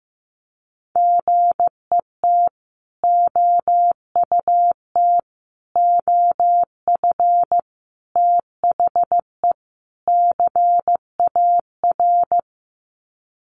Morse code, heard as a keyed tone, about 15 wpm, 700 Hz. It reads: GET OUT OF THE CAR
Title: Sets of 5 Words Encore2 15wpm